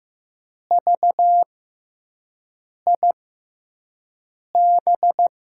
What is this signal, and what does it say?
Morse code, heard as a keyed tone, about 15 wpm, spaced as for 5 wpm, 700 Hz. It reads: VIB
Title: USA Call Signs - 5 Characters - 6x Character Spacing - 15wpm